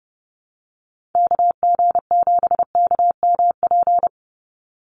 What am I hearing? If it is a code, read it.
Morse code, heard as a keyed tone, about 30 wpm, 700 Hz. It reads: KG7KMP
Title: USA Call Signs 30wpm